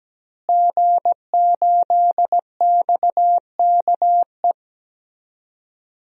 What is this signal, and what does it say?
Morse code, heard as a keyed tone, about 17 wpm, 700 Hz. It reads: G8XKE
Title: DXCC-100 Call Signs - 5 Characters - Repeated 2x - 17wpm